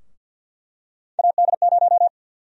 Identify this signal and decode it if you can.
Morse code, heard as a keyed tone, about 47 wpm, 700 Hz. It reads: AD0
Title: USA Call Sign Prefixes - 50wpm